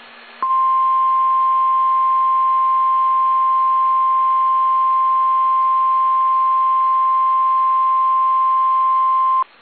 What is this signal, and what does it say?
Emergency Alert System (SAME) tones (the 1050 Hz attention tone). It is heard from 0.9 to 9.2 s.
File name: eas / KSDZKDJL / KSDZKDJL